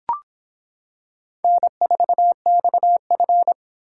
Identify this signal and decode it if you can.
Morse code, heard as a keyed tone, about 26 wpm, 700 Hz. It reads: N4XF